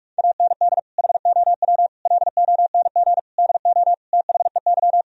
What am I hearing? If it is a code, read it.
Morse code, heard as a keyed tone, about 46 wpm, 700 Hz. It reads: AND HOW LONG DO THEY
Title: Sentences from Top 1000 Words - Review Part 4 - 45wpm